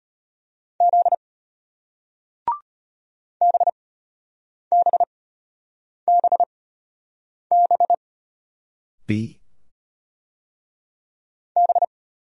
Morse code, heard as a keyed tone, about 33 wpm, spaced as for 8 wpm, 700 Hz. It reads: Z BBBB B